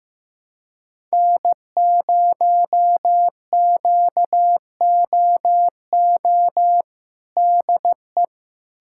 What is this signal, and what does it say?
Morse code, heard as a keyed tone, about 15 wpm, 700 Hz. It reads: N0QOO DE
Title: USA Callsign DE Callsign - 5 Characters - 15wpm